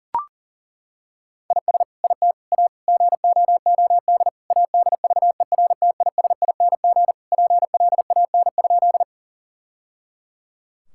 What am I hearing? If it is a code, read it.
Morse code, heard as a keyed tone, about 37 wpm, 700 Hz. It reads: IS IT A GOOD ADVERTISING PLAN?
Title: Sentences from Top 1000 Words - Review Part 6 - 40wpm